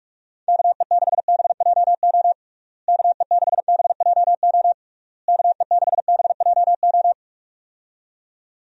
Morse code, heard as a keyed tone, about 41 wpm, 700 Hz. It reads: KE6BJO KE6BJO KE6BJO
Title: USA Call Signs - 6 Characters - Repeated 3x - 45wpm